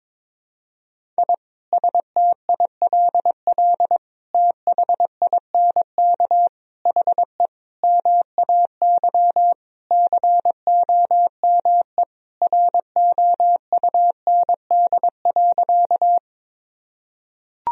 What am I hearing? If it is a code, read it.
Morse code, heard as a keyed tone, about 22 wpm, 700 Hz. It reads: I STILL THINK HE MAY COME ROUND.
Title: Sentences from Top 1000 Words - Review Part 1 - 22wpm